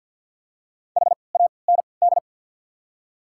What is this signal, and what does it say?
Morse code, heard as a keyed tone, about 52 wpm, spaced as for 29 wpm, 700 Hz. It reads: HAND
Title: 3x Character Spacing - Top 200 Words - 50wpm